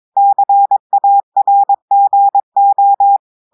Morse code, heard as a keyed tone, about 22 wpm, 800 Hz. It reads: CARGO
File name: audios / portugais / portugais texte en clair 1 vitesse 22 mots minutes